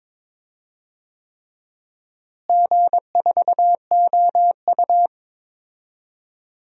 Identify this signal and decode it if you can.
Morse code, heard as a keyed tone, about 22 wpm, 700 Hz. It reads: G4OU
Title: DXCC-100 Call Signs - 4 Characters - Repeated 3x - 22wpm